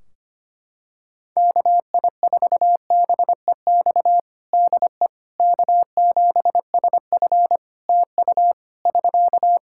Morse code, heard as a keyed tone, about 25 wpm, 700 Hz. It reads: KI4BEX DE K7SF TU <SK>
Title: Practice QSO - Exchange 4 - Ending - 25wpm